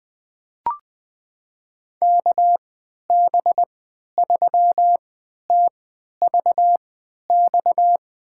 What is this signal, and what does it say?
Morse code, heard as a keyed tone, about 20 wpm, spaced as for 11 wpm, 700 Hz. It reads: KB3TVX
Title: USA Call Signs - 6 Characters - 3x Character Spacing - 20wpm